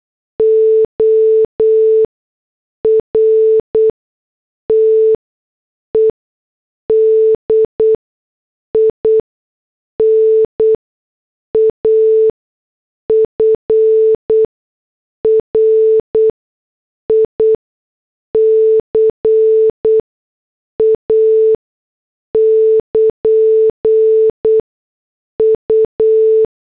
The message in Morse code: ORTED IN AFRICA <KN> U